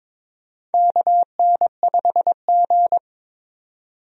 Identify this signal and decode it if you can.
Morse code, heard as a keyed tone, about 22 wpm, 700 Hz. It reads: KN5G
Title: USA Call Signs - 4 Characters - 22wpm